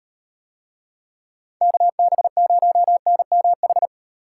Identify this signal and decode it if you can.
Morse code, heard as a keyed tone, about 38 wpm, 700 Hz. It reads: KB0NMH